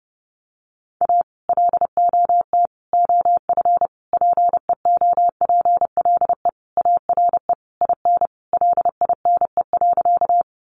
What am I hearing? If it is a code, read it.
Morse code, heard as a keyed tone, about 30 wpm, 700 Hz. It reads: A LOT OF PEOPLE ARE IN LINE.
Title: Additional Easy Phrases 30wpm